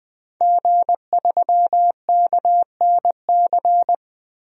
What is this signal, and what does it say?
Morse code, heard as a keyed tone, about 20 wpm, 700 Hz. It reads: G3KNC
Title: DXCC-100 Call Signs - 5 Characters - Repeated 3x - 20wpm